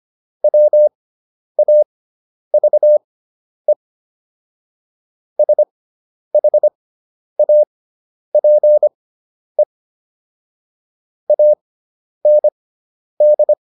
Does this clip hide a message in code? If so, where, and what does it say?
Yes: Morse code from 0.4 to 13.5 s, heard as a keyed tone, about 25 wpm, spaced as for 10 wpm, 600 Hz. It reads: WAVE SHAPE AND